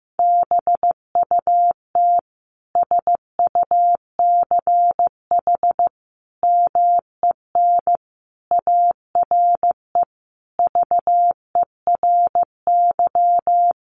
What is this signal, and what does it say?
Morse code, heard as a keyed tone, about 15 wpm, 700 Hz. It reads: BUT SUCH MEN ARE VERY
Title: Sentences from Top 1100 Words 15wpm